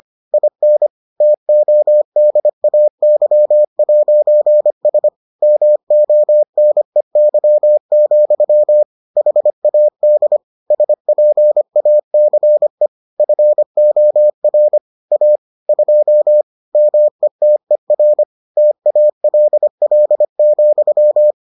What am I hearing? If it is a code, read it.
Morse code, heard as a keyed tone, about 25 wpm, 600 Hz. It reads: IN TODAY'S MONEY, HAD SPACE FOR A 2 METER TALL,